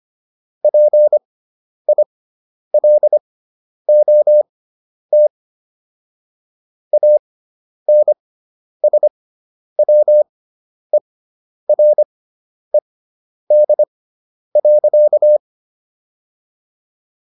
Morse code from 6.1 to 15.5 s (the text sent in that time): ANSWERED.